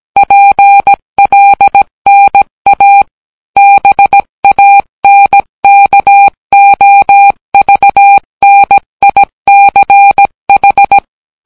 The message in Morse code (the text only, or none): PLNA BANKOVNICH